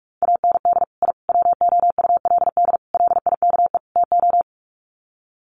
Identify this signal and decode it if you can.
Morse code, heard as a keyed tone, about 44 wpm, 700 Hz. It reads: AND I WOULD LIKE TO
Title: Sets of 5 Words Encore 45wpm